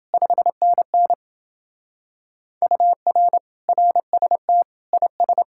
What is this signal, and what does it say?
Morse code, heard as a keyed tone, about 29 wpm, 700 Hz. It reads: 5NN UR RST IS